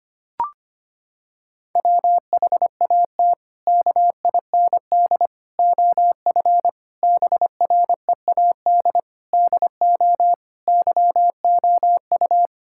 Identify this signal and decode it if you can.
Morse code, heard as a keyed tone, about 25 wpm, 700 Hz. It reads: WHAT KIND OF BREAD DO YOU
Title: Sentences from Top 1600 Words 25wpm